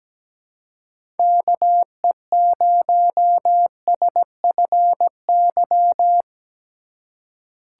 Morse code, heard as a keyed tone, about 17 wpm, 700 Hz. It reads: KE0SFY
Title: USA Call Signs - 6 Characters - 2x Character Spacing - 17wpm